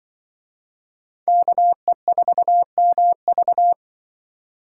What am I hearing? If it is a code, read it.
Morse code, heard as a keyed tone, about 24 wpm, 700 Hz. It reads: KE4MV
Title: Speed Racing - USA 2x2 Call Signs 20wpm